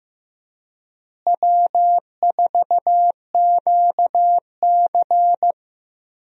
Morse code, heard as a keyed tone, about 16 wpm, 700 Hz. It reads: W4QC
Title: USA Call Signs - 4 Characters - Repeated 2x - 15wpm